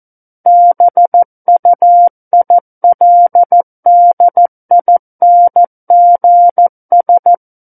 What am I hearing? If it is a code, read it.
Morse code, heard as a keyed tone, about 14 wpm, 700 Hz. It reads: BUILDINGS
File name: Book One / 05-Chapter 5 - 14 WPM